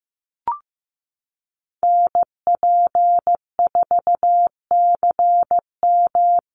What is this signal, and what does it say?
Morse code, heard as a keyed tone, about 15 wpm, 700 Hz. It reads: NP4CM
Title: USA Call Signs Repeated 3x 15wpm